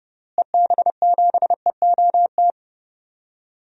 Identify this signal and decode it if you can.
Morse code, heard as a keyed tone, about 30 wpm, 700 Hz. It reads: EB7EOT
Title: DXCC-100 Call Signs - 6 Characters - Repeated 3x - 30wpm